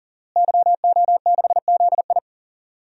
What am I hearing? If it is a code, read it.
Morse code, heard as a keyed tone, about 40 wpm, 700 Hz. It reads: YO6ZI
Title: DXCC-100 Call Signs - 5 Characters - Repeated 2x - 40wpm